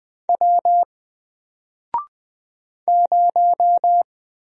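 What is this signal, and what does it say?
Morse code, heard as a keyed tone, about 20 wpm, 700 Hz. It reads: W 0
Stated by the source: Single letter and number Morse drill at 20 wpm, from N4LCD.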